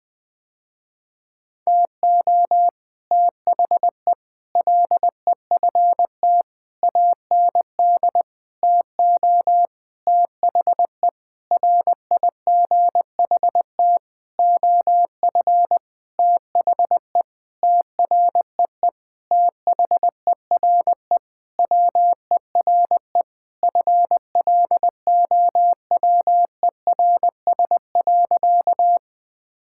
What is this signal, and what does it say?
Morse code, heard as a keyed tone, about 20 wpm, 700 Hz. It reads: TO THE LEFT AND TO THE RIGHT OF THE TREE THERE WERE FLOWERS.
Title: Sentences with 9-Word Phrases - Spoken Beginning - 20wpm